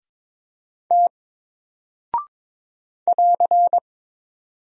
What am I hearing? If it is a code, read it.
Morse code, heard as a keyed tone, about 22 wpm, 700 Hz. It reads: T <AR>
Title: Single Character (Letter:Number:Punctuation:Prosign) 22wpm